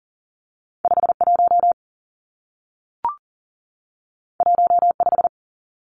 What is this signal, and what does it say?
Morse code, heard as a keyed tone, about 40 wpm, 700 Hz. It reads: 51 15
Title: Numbers - 2-Digit 40wpm